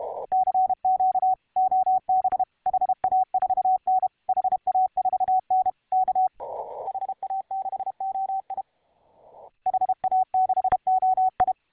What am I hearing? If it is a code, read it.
Morse code, heard as a keyed tone, about 32 wpm, 750 Hz. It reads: CQ OB HA4N HA4N K HA6OI HA6OI